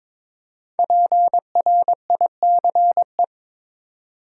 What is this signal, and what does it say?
Morse code, heard as a keyed tone, about 22 wpm, 700 Hz. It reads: PRICE
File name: 5 Letter Words - First 3 Letters Given - 22wpm